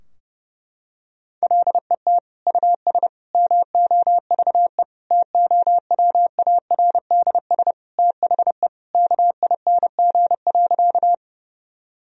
Morse code, heard as a keyed tone, about 30 wpm, 700 Hz. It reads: LET US MOVE TOWARDS THE KING.
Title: Sentences from Top 1000 Words - Review Part 2 - 30wpm